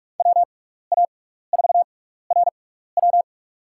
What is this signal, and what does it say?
Morse code, heard as a keyed tone, about 43 wpm, spaced as for 15 wpm, 700 Hz. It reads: WA4RW
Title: USA Call Signs - 5 Characters - 6x Character Spacing - 45wpm